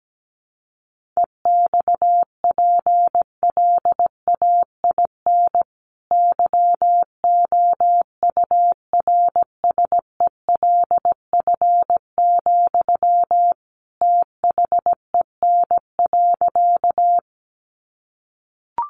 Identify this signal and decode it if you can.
Morse code, heard as a keyed tone, about 17 wpm, 700 Hz. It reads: EXPLAIN YOURSELF, THEN.
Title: Sentences from Top 1100 Words 17wpm